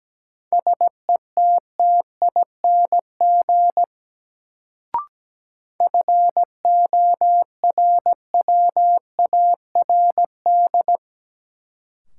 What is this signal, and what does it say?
Morse code, heard as a keyed tone, about 17 wpm, 700 Hz. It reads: SETTING FORWARD
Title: Sentences from Top 1300 Words - New Word Drill - 17wpm